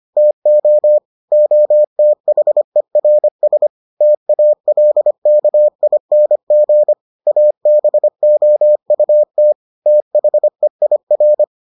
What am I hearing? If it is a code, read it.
Morse code, heard as a keyed tone, about 25 wpm, 600 Hz. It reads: TO OTHERS TALKING ABOUT THEIR